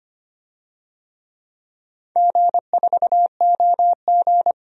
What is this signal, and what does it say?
Morse code, heard as a keyed tone, about 25 wpm, 700 Hz. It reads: G4OG
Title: DXCC-100 Call Signs - 4 Characters - Repeated 3x - 25wpm